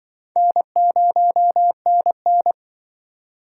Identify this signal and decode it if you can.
Morse code, heard as a keyed tone, about 25 wpm, 700 Hz. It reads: N0NN